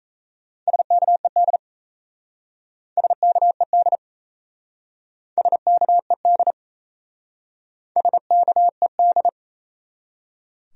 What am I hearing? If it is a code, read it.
Morse code, heard as a keyed tone, about 36 wpm, 700 Hz. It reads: SKED SKED SKED SKED